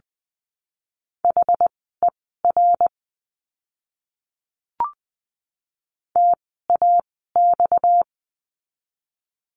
Morse code, heard as a keyed tone, about 20 wpm, spaced as for 15 wpm, 700 Hz. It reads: HER TAX
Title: 230 Lesson 37f - Letter x - Words 20wpm